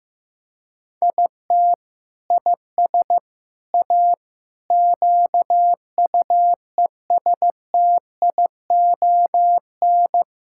Morse code, heard as a keyed tone, about 15 wpm, 700 Hz. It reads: IT IS A QUESTION
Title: Sentences from Top 1000 Words - Review Part 6 - 15wpm